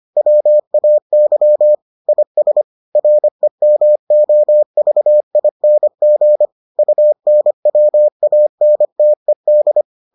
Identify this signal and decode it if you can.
Morse code, heard as a keyed tone, about 25 wpm, 600 Hz. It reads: WAY IS REMOVING UNWANTED